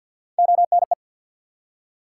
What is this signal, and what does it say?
Morse code, heard as a keyed tone, about 53 wpm, 700 Hz. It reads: ONE
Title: Three Letter Words - 50wpm